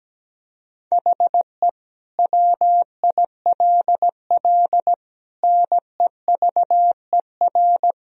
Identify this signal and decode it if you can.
Morse code, heard as a keyed tone, about 17 wpm, 700 Hz. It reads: HE WILL NEVER
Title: Sentences from Top 1000 Words - Review Part 2 - 17wpm